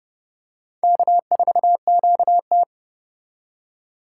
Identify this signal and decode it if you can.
Morse code, heard as a keyed tone, about 30 wpm, 700 Hz. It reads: K4QT